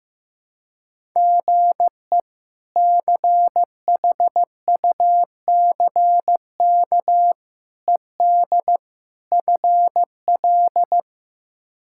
Morse code, heard as a keyed tone, about 15 wpm, 700 Hz. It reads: GE CHUCK ED FL
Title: K1USN SST QSO Practice - 15wpm